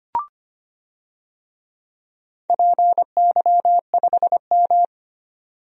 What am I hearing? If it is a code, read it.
Morse code, heard as a keyed tone, about 25 wpm, 700 Hz. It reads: PY5M
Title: DXCC-100 Call Signs - 4 Characters - 25wpm